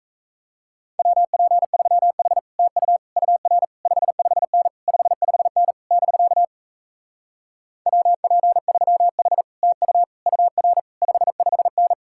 Morse code, heard as a keyed tone, about 38 wpm, 700 Hz. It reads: WP3H TU UR 55N 55N <BK> WP3H TU UR 55N